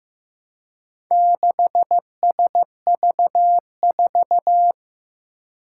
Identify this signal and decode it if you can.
Morse code, heard as a keyed tone, about 15 wpm, 700 Hz. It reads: 6SV4
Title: Four Challenging Letter-Number Combinations 15wpm